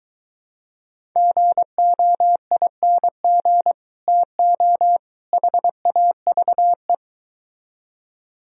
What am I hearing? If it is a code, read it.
Morse code, heard as a keyed tone, about 22 wpm, 700 Hz. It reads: GOING TO HAVE